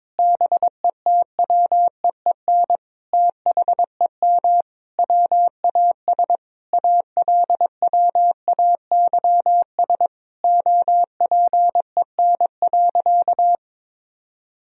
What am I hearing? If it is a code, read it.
Morse code, heard as a keyed tone, about 22 wpm, 700 Hz. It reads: BETWEEN THEM WAS ALWAYS OPEN.
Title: Sentences from Top 1000 Words - Review Part 1 - 22wpm